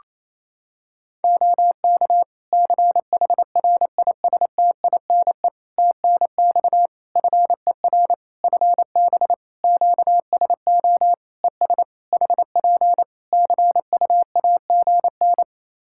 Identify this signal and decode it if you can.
Morse code, heard as a keyed tone, about 28 wpm, 700 Hz. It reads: OK CHRISTINE TNX FER FB QSO ES HP CUAGN
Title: Practice QSO - Exchange 4 - Ending - 28wpm